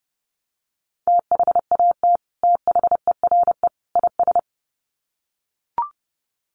Morse code, heard as a keyed tone, about 30 wpm, 700 Hz. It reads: THAT THERE IS
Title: Sets of 3 Words 30wpm